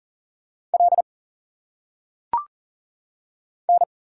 Morse code, heard as a keyed tone, about 38 wpm, 700 Hz. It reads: L N
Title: Single Letters - Phonetic Pronunciation 40wpm